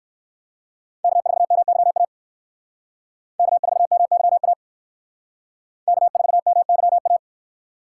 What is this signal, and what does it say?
Morse code, heard as a keyed tone, about 57 wpm, 700 Hz. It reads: K4MYA K4MYA K4MYA